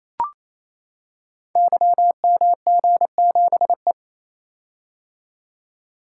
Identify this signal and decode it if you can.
Morse code, heard as a keyed tone, about 28 wpm, 700 Hz. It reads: YMG7E
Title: Code Groups - 5 Letter:Numbers - 28wpm